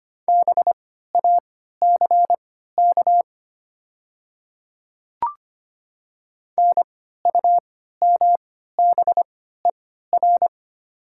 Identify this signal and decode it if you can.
Morse code, heard as a keyed tone, about 25 wpm, spaced as for 14 wpm, 700 Hz. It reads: BACK NUMBER
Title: 3x Character Spacing - Top 200 Words - 25wpm